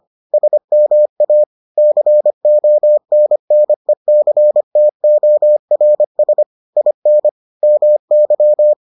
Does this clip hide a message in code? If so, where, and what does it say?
Yes: Morse code from 0.3 to 8.7 s, heard as a keyed tone, about 25 wpm, 600 Hz. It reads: SMA CONNECTORS IN MY